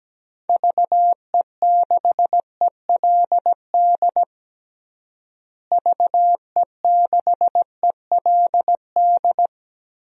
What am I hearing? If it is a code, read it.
Morse code, heard as a keyed tone, about 17 wpm, 700 Hz. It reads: VE6ELD VE6ELD
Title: DXCC-100 Call Signs - 6 Characters - Repeated 3x - 17wpm